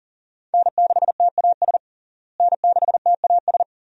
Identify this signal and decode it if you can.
Morse code, heard as a keyed tone, about 40 wpm, 700 Hz. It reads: N6TAS N6TAS
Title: USA Call Signs - 5 Characters - Repeated 2x - 40wpm